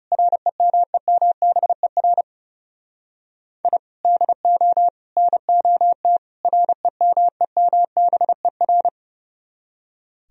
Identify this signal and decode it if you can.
Morse code, heard as a keyed tone, about 38 wpm, 700 Hz. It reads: REMEMBER I DO NOT REMEMBER